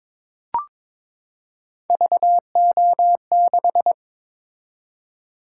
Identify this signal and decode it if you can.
Morse code, heard as a keyed tone, about 22 wpm, 700 Hz. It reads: VO6
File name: practice / 22wpm / Three Letter-Number Combinations - 22wpm